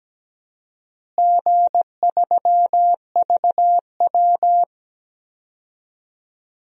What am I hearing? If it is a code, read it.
Morse code, heard as a keyed tone, about 17 wpm, 700 Hz. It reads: G3VW
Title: DXCC-100 Call Signs - 4 Characters - Repeated 3x - 17wpm